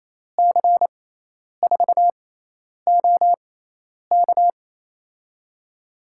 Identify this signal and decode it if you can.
Morse code, heard as a keyed tone, about 28 wpm, spaced as for 10 wpm, 700 Hz. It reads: C4OK
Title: USA Call Signs - 6x Character Spacing - 28wpm